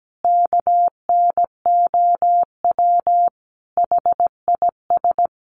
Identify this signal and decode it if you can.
Morse code, heard as a keyed tone, about 17 wpm, 700 Hz. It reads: KNOW HIS
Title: Sentences from Top 800 Words 17wpm